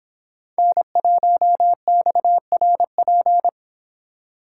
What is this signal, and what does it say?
Morse code, heard as a keyed tone, about 26 wpm, 700 Hz. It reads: N1XRP
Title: Speed Racing - USA 1x3 Call Signs 22wpm